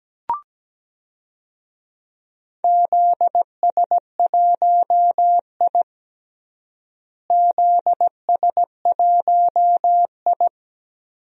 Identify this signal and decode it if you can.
Morse code, heard as a keyed tone, about 17 wpm, 700 Hz. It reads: ZS1I ZS1I
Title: DXCC-100 Call Signs - 4 Characters - Repeated 3x - 17wpm